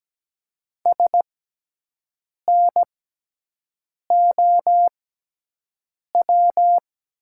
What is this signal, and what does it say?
Morse code, heard as a keyed tone, about 17 wpm, spaced as for 6 wpm, 700 Hz. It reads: SNOW
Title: 6x Character Spacing - Top 500 Words - 17wpm